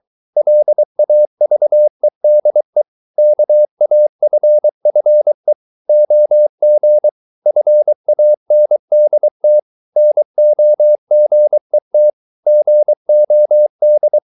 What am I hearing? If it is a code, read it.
Morse code, heard as a keyed tone, about 23 wpm, 600 Hz. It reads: LAVEDE KAFFE OG FANDT NOGET GOD
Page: Ferie 23 wpm